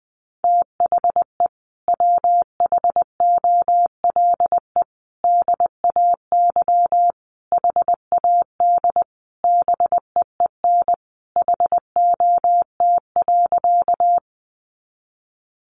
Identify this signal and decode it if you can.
Morse code, heard as a keyed tone, about 20 wpm, 700 Hz. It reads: THE WHOLE DAY HAD BEEN HOT.
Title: Sentences from Top 500 Words 20wpm